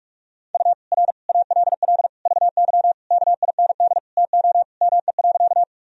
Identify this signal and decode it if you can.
Morse code, heard as a keyed tone, about 46 wpm, 700 Hz. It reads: U R ALL VY KIND TO ME.
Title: Sentences from Top 200 Words and Abbreviations 45wpm